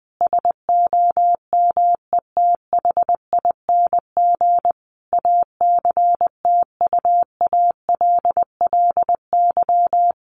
Morse code, heard as a keyed tone, about 19 wpm, 700 Hz. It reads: SOMETHING ACTUALLY
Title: Sentences from Top 1500 Words 20wpm